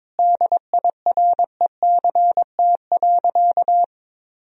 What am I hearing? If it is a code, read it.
Morse code, heard as a keyed tone, about 22 wpm, 700 Hz. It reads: DIRECT.
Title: Sentences from Top 1000 Words - Review Part 2 - 22wpm